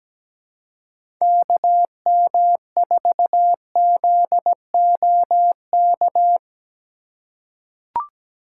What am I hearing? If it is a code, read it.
Morse code, heard as a keyed tone, about 17 wpm, 700 Hz. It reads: KM4ZOK
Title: USA Call Signs - 6 Characters - 2x Character Spacing - 17wpm